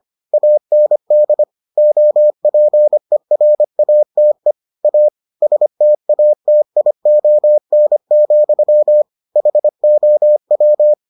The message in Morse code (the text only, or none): AND OPERATE A STATION, HOW